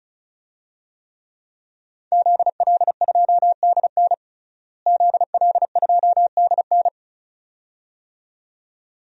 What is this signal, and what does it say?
Morse code, heard as a keyed tone, about 34 wpm, 700 Hz. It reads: ZL2DN ZL2DN
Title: DXCC-100 Call Signs - 5 Characters - Repeated 2x - 35wpm